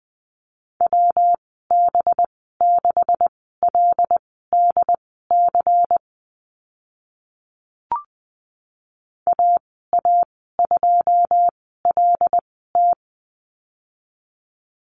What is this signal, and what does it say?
Morse code, heard as a keyed tone, about 20 wpm, spaced as for 15 wpm, 700 Hz. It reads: WB6LDC AA2LT
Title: 193 Lesson 31g - Letter v - Call Signs 20wpm